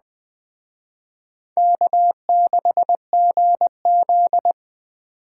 Morse code, heard as a keyed tone, about 20 wpm, 700 Hz. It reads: K6GZ
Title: Speed Racing - USA Call Signs 17wpm